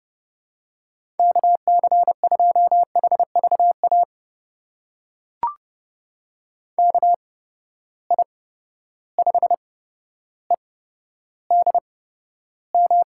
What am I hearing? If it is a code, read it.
Morse code, heard as a keyed tone, about 32 wpm, spaced as for 13 wpm, 700 Hz. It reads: KC2HVA KI5EDM
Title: USA Call Signs - 6 Characters - 8x Character Spacing - 30wpm